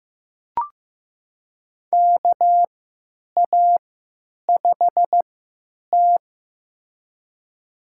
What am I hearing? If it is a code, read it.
Morse code, heard as a keyed tone, about 16 wpm, spaced as for 9 wpm, 700 Hz. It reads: KA5T
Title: USA Call Signs - 4 Characters - 3x Character Spacing - 15wpm